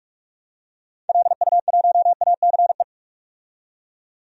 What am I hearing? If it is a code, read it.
Morse code, heard as a keyed tone, about 42 wpm, 700 Hz. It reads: LU1ACE
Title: DXCC-100 Call Signs - 6 Characters - 45wpm